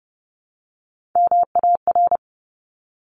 Morse code, heard as a keyed tone, about 30 wpm, 700 Hz. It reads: MAR
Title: ARRL Sections 30wpm